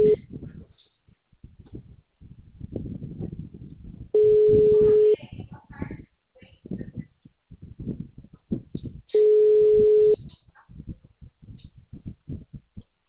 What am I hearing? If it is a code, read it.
Ringback tone (Europe, 425 Hz).